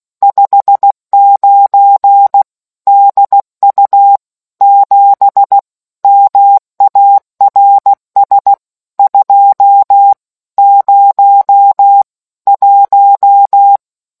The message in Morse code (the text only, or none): 59 DU 7 MARS 201